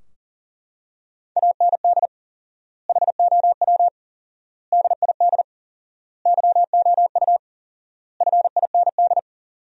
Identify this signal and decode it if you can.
Morse code, heard as a keyed tone, about 40 wpm, 700 Hz. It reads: AND HOW DID YOU FIND
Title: 4x Word-Spacing - Sentences from Top 100 Words - 40wpm